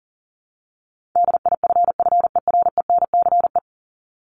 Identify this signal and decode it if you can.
Morse code, heard as a keyed tone, about 37 wpm, 700 Hz. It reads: DIFFERENCE
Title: Top 751-1000 Words 40wpm